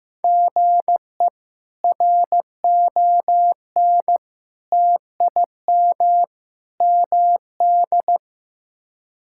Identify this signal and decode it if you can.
Morse code, heard as a keyed tone, about 15 wpm, 700 Hz. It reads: GE RON TIM MD
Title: K1USN SST QSO Practice - 15wpm